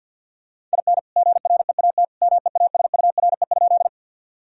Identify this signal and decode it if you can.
Morse code, heard as a keyed tone, about 50 wpm, 700 Hz. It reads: IN GREAT MEASURE?
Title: Sentences from Top 1000 Words - Review Part 2 - 50wpm